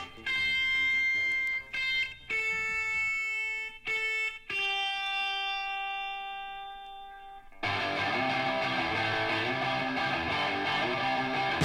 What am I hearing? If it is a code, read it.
Dial-up modem handshake.